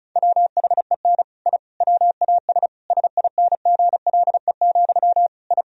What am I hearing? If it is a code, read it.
Morse code, heard as a keyed tone, about 34 wpm, 700 Hz. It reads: WHEN I WAS SINGLE, I
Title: Sentences from Top 1000 Words - Review Part 4 - 35wpm